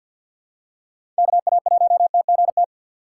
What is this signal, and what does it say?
Morse code, heard as a keyed tone, about 50 wpm, 700 Hz. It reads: KA1TGT
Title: USA Call Signs - 6 Characters - 6x Character Spacing - 50wpm